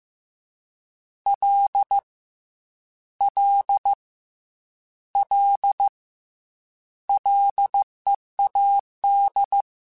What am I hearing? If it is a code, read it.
Morse code, heard as a keyed tone, about 15 wpm, 800 Hz. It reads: L L L LEAD
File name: Ls / Single L m3s / Single L Prefixe 3x2 Twice @13